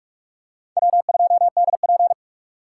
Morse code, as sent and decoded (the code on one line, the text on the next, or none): .-- ..--- -.. .--.
W2DP